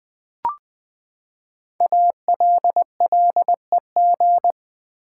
Morse code, heard as a keyed tone, about 20 wpm, 700 Hz. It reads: ALLEG
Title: Build A Word - 5 Letters At A Time - 10-Letter Words - 20wpm